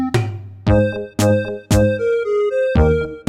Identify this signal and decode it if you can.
There is no code.